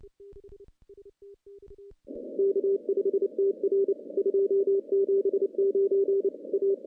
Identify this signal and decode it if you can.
Morse code, heard as a keyed tone, about 30 wpm, 390 Hz. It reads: K5TR 279 A
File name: contests / CW-SS-k5tr- -0700